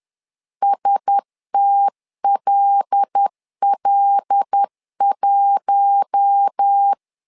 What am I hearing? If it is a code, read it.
Morse code, heard as a keyed tone, about 10 wpm, 800 Hz. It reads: STLL1
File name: LEZIONI / 5 LBVC / 5 L 10wpm 2259